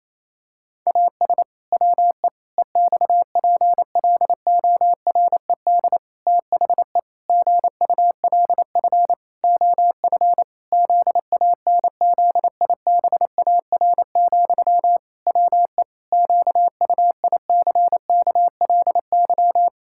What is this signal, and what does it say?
Morse code, heard as a keyed tone, about 28 wpm, 700 Hz. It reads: AS WE EXPLORED THE GULF OF ZANZIBAR, WE QUICKLY
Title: Pangram Challenge v2 - 28wpm